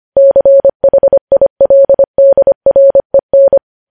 Morse code, heard as a keyed tone, about 25 wpm, 550 Hz. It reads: CHILDREN
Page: news-headlines-in-morse-code-at-25-wpm